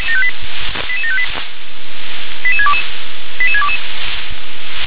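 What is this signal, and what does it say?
Five-tone selective call (ZVEI), heard as quick runs of five tones, tones of about 70 ms. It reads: 88758 88622 88622
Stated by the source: Five-tone ZVEI selective call received by SDR.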